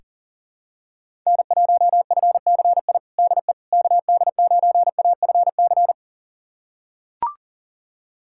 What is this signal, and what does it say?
Morse code, heard as a keyed tone, about 40 wpm, 700 Hz. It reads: N1FCI DE KD9AFC
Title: USA Callsign DE Callsign 40wpm